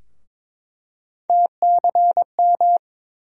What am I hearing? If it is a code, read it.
Morse code, heard as a keyed tone, about 22 wpm, 700 Hz. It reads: TCM
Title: USA Call Sign Postfixes - 3 Letters 22wpm